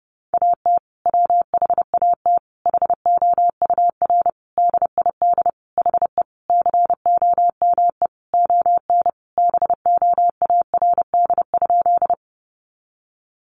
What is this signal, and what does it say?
Morse code, heard as a keyed tone, about 30 wpm, 700 Hz. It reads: AT WHAT HOUR DID HE COME ON BOARD?
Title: Sentences from Top 700 Words 30wpm